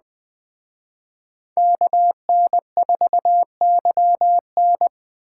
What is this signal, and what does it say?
Morse code, heard as a keyed tone, about 20 wpm, 700 Hz. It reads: KN4YN